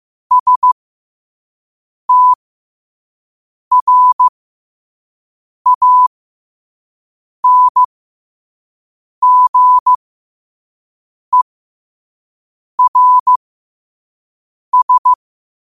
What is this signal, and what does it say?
Morse code, heard as a keyed tone, about 15 wpm, spaced as for 5 wpm, 1000 Hz. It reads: STRANGERS
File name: qotdfiles / 5wpm